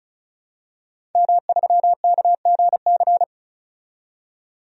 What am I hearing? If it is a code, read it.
Morse code, heard as a keyed tone, about 36 wpm, 700 Hz. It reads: M3KGC
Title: DXCC-100 Call Signs - 5 Characters - 35wpm